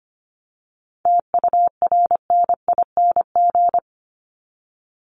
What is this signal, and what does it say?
Morse code, heard as a keyed tone, about 25 wpm, 700 Hz. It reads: TURNING
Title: ING Suffix - Top 100 Words with Base Words 25wpm